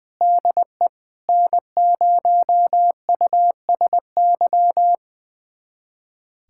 Morse code, heard as a keyed tone, about 20 wpm, 700 Hz. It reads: DE N0USY
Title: USA Callsign DE Callsign - 5 Characters - 20wpm